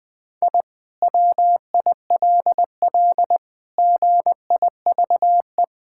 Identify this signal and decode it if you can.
Morse code, heard as a keyed tone, about 19 wpm, 700 Hz. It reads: I WILL GIVE
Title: Sentences from Top 1000 Words - Review Part 6 - 20wpm